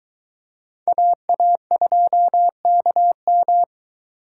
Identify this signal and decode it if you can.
Morse code, heard as a keyed tone, about 23 wpm, 700 Hz. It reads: AA2KM